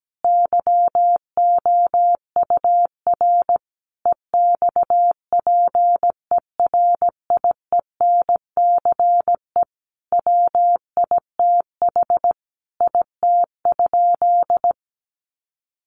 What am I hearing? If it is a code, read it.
Morse code, heard as a keyed tone, about 17 wpm, 700 Hz. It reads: YOUR EXPERIENCE WITH IT?
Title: Sentences from Top 600 Words 17wpm